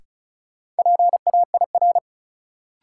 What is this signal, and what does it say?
Morse code, heard as a keyed tone, about 36 wpm, 700 Hz. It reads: PAIR
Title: Sentences from Top 1200 Words - New Word Drill Alt - 35wpm